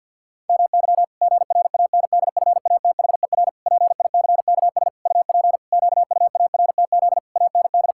Morse code, heard as a keyed tone, about 50 wpm, 700 Hz. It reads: MY GRANDFATHER PICKS UP QUARTZ AND
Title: Pangram Challenge v2 - 50wpm